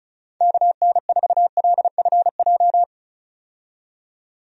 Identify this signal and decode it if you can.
Morse code, heard as a keyed tone, about 34 wpm, 700 Hz. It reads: KN4LFJ
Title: USA Call Signs - 6 Characters - 35wpm